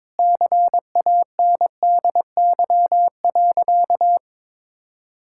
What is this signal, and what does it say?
Morse code, heard as a keyed tone, about 22 wpm, 700 Hz. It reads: CANDY.
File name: practice / 22wpm / Sentences with 4-Word Phrases - 22wpm